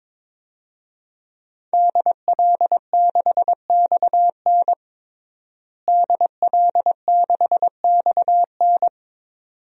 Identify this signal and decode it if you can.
Morse code, heard as a keyed tone, about 22 wpm, 700 Hz. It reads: DL6XN DL6XN
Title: DXCC-100 Call Signs - 5 Characters - Repeated 2x - 22wpm